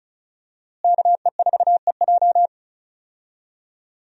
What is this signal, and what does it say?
Morse code, heard as a keyed tone, about 36 wpm, 700 Hz. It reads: KE4EJ
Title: USA Call Signs - 5 Characters - Repeated 3x - 35wpm